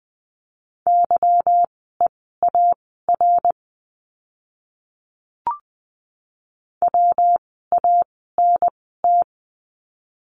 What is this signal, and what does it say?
Morse code, heard as a keyed tone, about 20 wpm, spaced as for 14 wpm, 700 Hz. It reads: YEAR WANT
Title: 230 Lesson 37f - Letter x - Words 20wpm